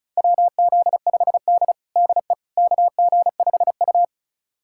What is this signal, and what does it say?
Morse code, heard as a keyed tone, about 36 wpm, 700 Hz. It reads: WZ5D DE KG5U
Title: USA Callsign DE Callsign - 4 Characters - 35wpm